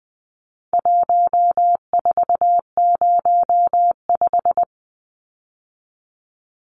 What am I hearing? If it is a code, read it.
Morse code, heard as a keyed tone, about 20 wpm, 700 Hz. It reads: 1405
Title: Numbers - 4-Digit 20wpm